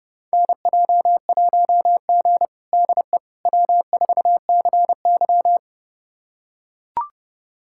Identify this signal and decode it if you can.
Morse code, heard as a keyed tone, about 30 wpm, 700 Hz. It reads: NJ1G DE W4CY
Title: USA Callsign DE Callsign - 4 Characters - 30wpm